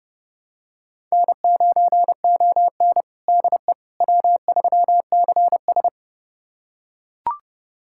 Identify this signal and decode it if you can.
Morse code, heard as a keyed tone, about 30 wpm, 700 Hz. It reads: N9ON DE W3CS
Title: USA Callsign DE Callsign - 4 Characters - 30wpm